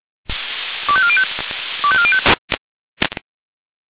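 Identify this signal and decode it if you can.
Five-tone selective call (ZVEI), heard as quick runs of five tones, tones of about 70 ms. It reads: 25405 25405
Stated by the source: ZVEI five-tone selective call received on an SDR.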